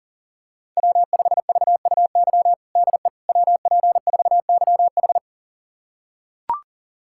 Morse code, heard as a keyed tone, about 40 wpm, 700 Hz. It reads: W5VUY DE WP4YH
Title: USA Callsign DE Callsign - 5 Characters - 40wpm